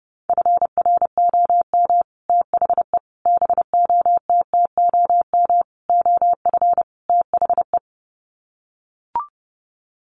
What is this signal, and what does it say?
Morse code, heard as a keyed tone, about 28 wpm, 700 Hz. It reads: FROM THE BOTTOM OF THE
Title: Sets of 5 Words Encore2 30wpm